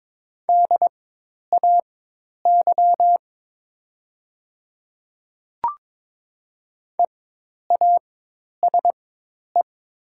Morse code, heard as a keyed tone, about 22 wpm, spaced as for 10 wpm, 700 Hz. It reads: DAY EASE
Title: 4x Character Spacing - Top 500 Words - 22wpm